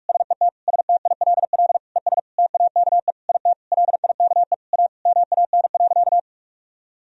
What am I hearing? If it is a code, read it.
Morse code, heard as a keyed tone, about 46 wpm, 700 Hz. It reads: SET STILL ES TAKE IT LIKE A MAN.
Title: Sentences from Top 300 Words and Abbreviations 45wpm